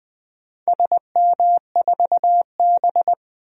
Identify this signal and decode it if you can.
Morse code, heard as a keyed tone, about 20 wpm, 700 Hz. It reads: SM4B
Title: DXCC-100 Call Signs - 4 Characters - 20wpm